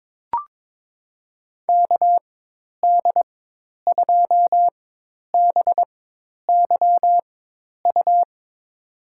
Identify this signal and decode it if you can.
Morse code, heard as a keyed tone, about 22 wpm, spaced as for 10 wpm, 700 Hz. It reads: KD2BYU
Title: USA Call Signs - 4x Character Spacing - 22wpm